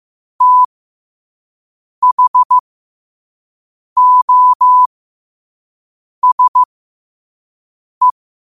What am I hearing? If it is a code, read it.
Morse code, heard as a keyed tone, about 15 wpm, spaced as for 5 wpm, 1000 Hz. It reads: THOSE
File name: qotdfiles / 5wpm